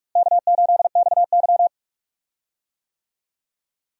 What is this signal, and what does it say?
Morse code, heard as a keyed tone, about 46 wpm, 700 Hz. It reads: K8XY
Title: USA Call Signs - 4 Characters - Repeated 3x - 45wpm